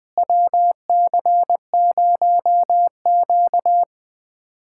Morse code, heard as a keyed tone, about 21 wpm, 700 Hz. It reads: WC0Q
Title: USA Callsign DE Callsign - 4 Characters - 20wpm